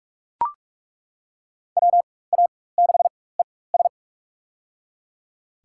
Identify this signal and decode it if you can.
Morse code, heard as a keyed tone, about 42 wpm, spaced as for 21 wpm, 700 Hz. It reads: WA6ES
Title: USA Call Signs - 5 Characters - 4x Character Spacing - 45wpm